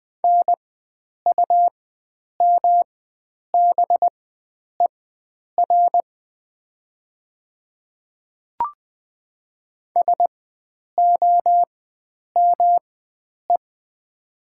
Morse code, heard as a keyed tone, about 20 wpm, spaced as for 9 wpm, 700 Hz. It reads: NUMBER SOME
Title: 4x Character Spacing - Top 100 Words - 20wpm